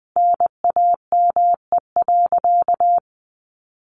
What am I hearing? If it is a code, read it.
Morse code, heard as a keyed tone, about 20 wpm, 700 Hz. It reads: NAME.
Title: Sentences from Top 500 Words 20wpm